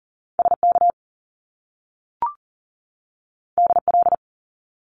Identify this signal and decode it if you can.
Morse code, heard as a keyed tone, about 40 wpm, 700 Hz. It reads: SK DL
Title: Top Two Letter Combinations in Words 40wpm